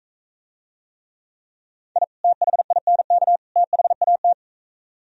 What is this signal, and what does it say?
Morse code, heard as a keyed tone, about 44 wpm, 700 Hz. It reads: I THINK THAT